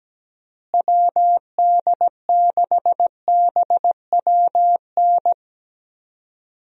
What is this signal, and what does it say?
Morse code, heard as a keyed tone, about 17 wpm, 700 Hz. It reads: WD6BWN
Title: USA Call Signs - 6 Characters - Repeated 2x - 17wpm